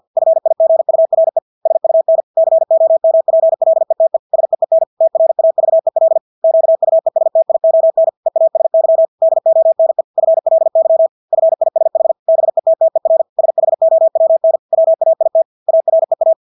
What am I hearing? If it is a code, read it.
Morse code, heard as a keyed tone, about 50 wpm, 650 Hz. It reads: FIGURE SUN COMPLETE SEEN TRAVEL QUESTION EASY DONE FLY FISH BETTER SHOWN WAIT AREA
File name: Engelske ord 50 wpm